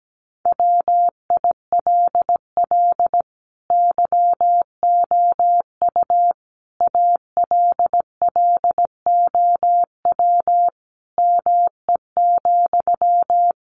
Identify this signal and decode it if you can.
Morse code, heard as a keyed tone, about 17 wpm, 700 Hz. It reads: WILL YOU ALLOW ME,
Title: Sentences from Top 700 Words 17wpm